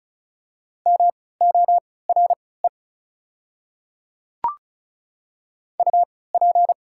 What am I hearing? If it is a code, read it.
Morse code, heard as a keyed tone, about 36 wpm, spaced as for 20 wpm, 690 Hz. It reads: MORE UP
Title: 3x Character Spacing - Top 100 Words - 35wpm